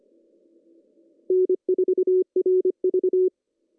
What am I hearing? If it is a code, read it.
Morse code, heard as a keyed tone, about 26 wpm, 380 Hz. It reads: N4RV
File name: CW / cqww-cw-k5tr